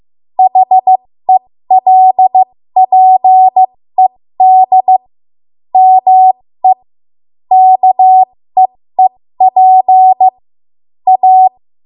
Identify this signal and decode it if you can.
Morse code, heard as a keyed tone, about 15 wpm, spaced as for 13 wpm, 750 Hz. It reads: HELPED ME KEEP A
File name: Morse / 13WPM